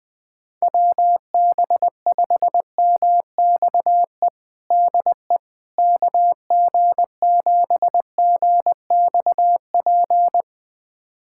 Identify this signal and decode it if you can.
Morse code, heard as a keyed tone, about 21 wpm, 700 Hz. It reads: WB5MXE DE KG7GXP
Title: USA Callsign DE Callsign - 6 Characters - 20wpm